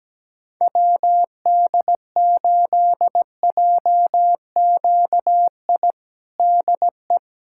Morse code, heard as a keyed tone, about 17 wpm, 700 Hz. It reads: WD8JQI DE
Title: USA Callsign DE Callsign 17wpm